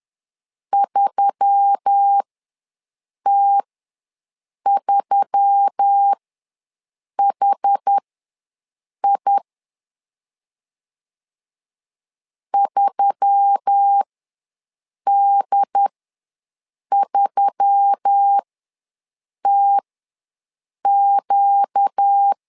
Morse code, heard as a keyed tone, about 10 wpm, spaced as for 6 wpm, 800 Hz. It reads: 3T3HI 3D3TQ
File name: LEZIONI / 4 Q1S3 ws 10wpm 2437